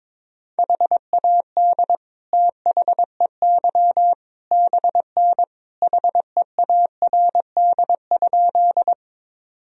Morse code, heard as a keyed tone, about 22 wpm, 700 Hz. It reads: HAD THEY BN HEARD?
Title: Sentences from Top 400 Words and Abbreviations 22wpm